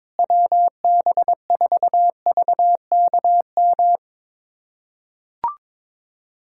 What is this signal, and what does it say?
Morse code, heard as a keyed tone, about 22 wpm, 700 Hz. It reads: WB4VKM
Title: USA Callsign DE Callsign - 6 Characters - 22wpm